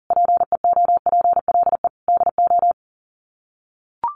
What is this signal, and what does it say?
Morse code, heard as a keyed tone, about 40 wpm, 700 Hz. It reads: PEOPLE DO
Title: Sets of 5 Words 40wpm